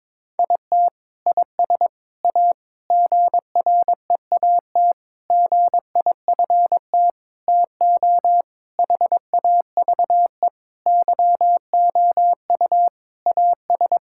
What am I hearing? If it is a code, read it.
Morse code, heard as a keyed tone, about 22 wpm, 700 Hz. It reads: IT IS A GREAT GIFT TO HAVE YOU AS
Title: Sentences from Top 1000 Words - Review Part 6 - 22wpm